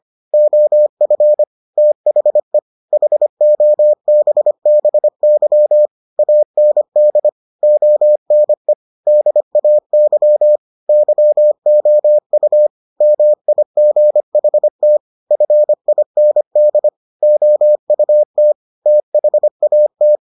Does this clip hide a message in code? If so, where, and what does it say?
Yes: Morse code from 0.3 to 20.2 s, heard as a keyed tone, about 25 wpm, 600 Hz. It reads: OF THE HOBBY AND ONE DAY YOU MIGHT FIND OUT THAT